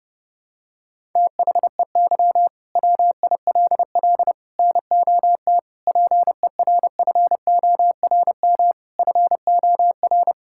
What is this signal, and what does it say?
Morse code, heard as a keyed tone, about 32 wpm, 700 Hz. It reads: THEY WILL NOT PERFORM FOR
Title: Sentences from Top 1000 Words - Review Part 5 - 30wpm